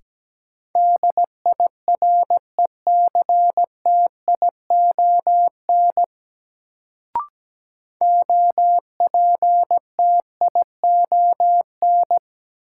Morse code, heard as a keyed tone, about 17 wpm, 700 Hz. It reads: DIRECTION OPTION
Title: Sentences from Top 1300 Words - New Word Drill Alt - 17wpm